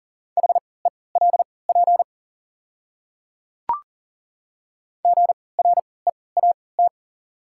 Morse code, heard as a keyed tone, about 40 wpm, spaced as for 23 wpm, 700 Hz. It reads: HELP GREAT
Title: 3x Character Spacing - Top 200 Words - 40wpm